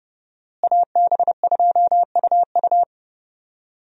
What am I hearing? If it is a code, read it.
Morse code, heard as a keyed tone, about 30 wpm, 700 Hz. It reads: AB2UU
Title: USA Call Signs - 5 Characters - Repeated 3x - 30wpm